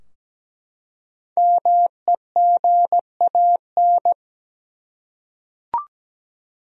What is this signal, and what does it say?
Morse code, heard as a keyed tone, about 17 wpm, 700 Hz. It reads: MEGAN
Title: USA Common Names - Max 5 Letters - 17wpm